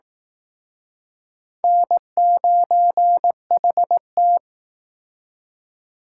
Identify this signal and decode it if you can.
Morse code, heard as a keyed tone, about 19 wpm, 700 Hz. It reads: N9HT